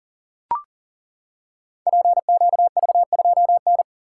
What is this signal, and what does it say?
Morse code, heard as a keyed tone, about 39 wpm, 690 Hz. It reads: PQV2N
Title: Code Groups - 5 Letter:Numbers - 40wpm